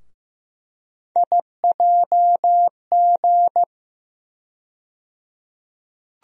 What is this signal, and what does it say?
Morse code, heard as a keyed tone, about 15 wpm, 700 Hz. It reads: IJG
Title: USA Call Sign Postfixes - 3 Letters 15wpm